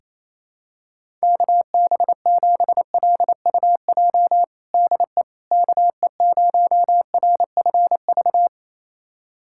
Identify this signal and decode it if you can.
Morse code, heard as a keyed tone, about 28 wpm, 700 Hz. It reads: KB7LUJ DE KE0RFV
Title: USA Callsign DE Callsign - 6 Characters - 28wpm